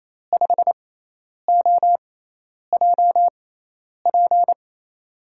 Morse code, heard as a keyed tone, about 28 wpm, spaced as for 10 wpm, 700 Hz. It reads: 5OJP
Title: USA Call Signs - 6x Character Spacing - 28wpm